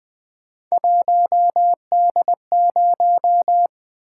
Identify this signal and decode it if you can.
Morse code, heard as a keyed tone, about 20 wpm, 700 Hz. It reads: 1D0
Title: Three Letter-Number Combinations - 20wpm